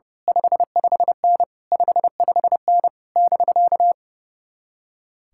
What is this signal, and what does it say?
Morse code, heard as a keyed tone, about 28 wpm, 700 Hz. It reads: 55N 55N <BK>
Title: Speed-Racing Common QSO Phrases - 30wpm